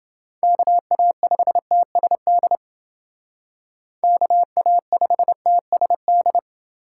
Morse code, heard as a keyed tone, about 29 wpm, 700 Hz. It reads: KA5TSD KA5TSD